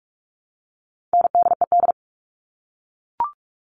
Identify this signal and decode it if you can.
Morse code, heard as a keyed tone, about 44 wpm, 700 Hz. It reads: NDED
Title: All Cut Numbers - 4-Digit 45wpm